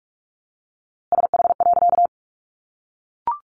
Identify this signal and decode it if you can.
Morse code, heard as a keyed tone, about 44 wpm, 700 Hz. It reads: SH.
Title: A Sound Of Thunder 45wpm